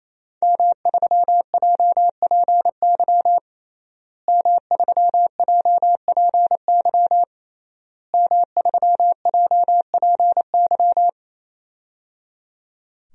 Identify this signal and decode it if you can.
Morse code, heard as a keyed tone, about 28 wpm, 700 Hz. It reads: M3JPY M3JPY M3JPY
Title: DXCC-100 Call Signs - 5 Characters - Repeated 3x - 28wpm